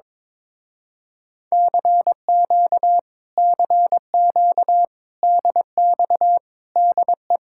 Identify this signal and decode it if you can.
Morse code, heard as a keyed tone, about 22 wpm, 700 Hz. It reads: CQ CQ DX DE
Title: Speed-Racing Common QSO Phrases Encore - 22wpm